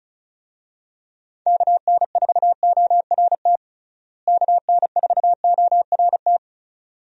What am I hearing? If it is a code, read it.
Morse code, heard as a keyed tone, about 36 wpm, 700 Hz. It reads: KN4ORT KN4ORT
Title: USA Call Signs - 6 Characters - Repeated 2x - 35wpm